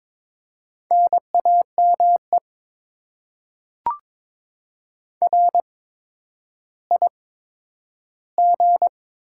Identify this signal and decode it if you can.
Morse code, heard as a keyed tone, about 22 wpm, spaced as for 11 wpm, 700 Hz. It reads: NAME RIG
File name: Top 20 QSO Elements - 8x Character Spacing - 22wpm